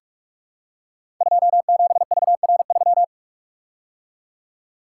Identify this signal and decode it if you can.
Morse code, heard as a keyed tone, about 44 wpm, 700 Hz. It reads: 27VR3
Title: Code Groups - 5 Letter:Numbers - 45wpm